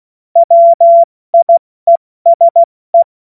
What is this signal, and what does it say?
Morse code, heard as a keyed tone, about 16 wpm, 680 Hz. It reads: WIESE
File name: Morsekurs / Witze / Witz 13. April 2021, 70 F80